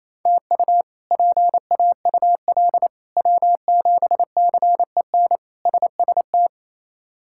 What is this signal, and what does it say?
Morse code, heard as a keyed tone, about 28 wpm, 700 Hz. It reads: TU PAUL W7CEN SST
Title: K1USN SST QSO Practice - 28wpm